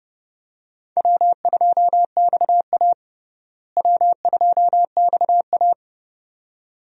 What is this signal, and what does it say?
Morse code, heard as a keyed tone, about 31 wpm, 700 Hz. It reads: W2XA W2XA
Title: USA Call Signs - 4 Characters - Repeated 2x - 30wpm